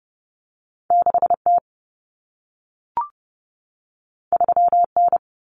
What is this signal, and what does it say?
Morse code, heard as a keyed tone, about 30 wpm, 700 Hz. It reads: 6T 3N
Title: Common Cut Numbers - 2-Digit 30wpm